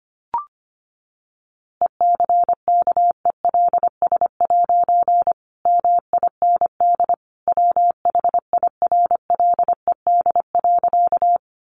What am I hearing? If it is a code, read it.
Morse code, heard as a keyed tone, about 25 wpm, 700 Hz. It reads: ECKELS' MIND WHIRLED.
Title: A Sound Of Thunder 25wpm